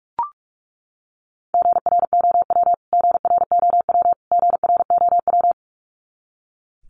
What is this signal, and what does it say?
Morse code, heard as a keyed tone, about 44 wpm, 700 Hz. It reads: GROW GROW GROW
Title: Top 500 Words Repeated 45wpm